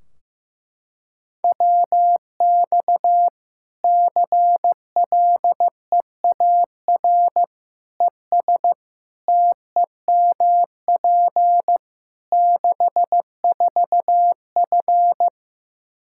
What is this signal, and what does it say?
Morse code, heard as a keyed tone, about 15 wpm, 700 Hz. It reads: WX CLEAR ES TEMP 64F
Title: K1USN SST QSO Practice 15wpm